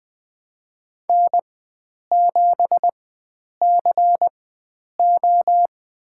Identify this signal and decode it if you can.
Morse code, heard as a keyed tone, about 20 wpm, spaced as for 9 wpm, 700 Hz. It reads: N7CO
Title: USA Call Signs - 4 Characters - 4x Character Spacing - 20wpm